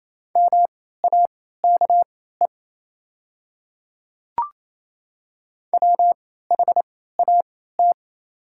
Morse code, heard as a keyed tone, about 28 wpm, spaced as for 16 wpm, 700 Hz. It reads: MAKE WHAT
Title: 3x Character Spacing - Top 100 Words - 28wpm